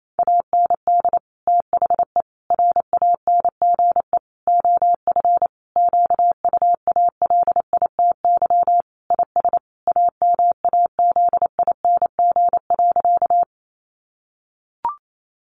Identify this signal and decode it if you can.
Morse code, heard as a keyed tone, about 28 wpm, 700 Hz. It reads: AND THE RANGE OF QUALITY IS AMAZING.
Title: Sentences from Top 1500 Words 28wpm